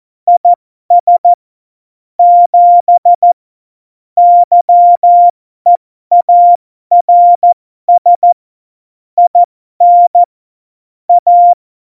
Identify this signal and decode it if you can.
Morse code, heard as a keyed tone, about 14 wpm, spaced as for 12 wpm, 700 Hz. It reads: IS 7 YEARS IN A
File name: Morsekurs / Witze